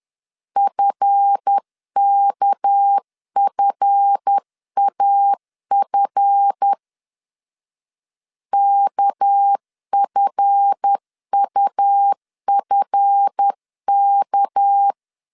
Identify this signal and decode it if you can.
Morse code, heard as a keyed tone, about 10 wpm, 800 Hz. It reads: FKFAF KFUFK
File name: LEZIONI / 7 GF9Z / 7 GF 10wpm 2514